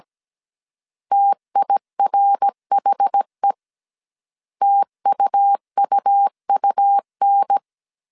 Morse code, heard as a keyed tone, about 17 wpm, 800 Hz. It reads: TIRHE TUUUN